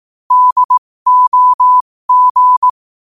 Morse code, heard as a keyed tone, about 18 wpm, spaced as for 15 wpm, 1000 Hz. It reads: DOG
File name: qotdfiles / 15wpm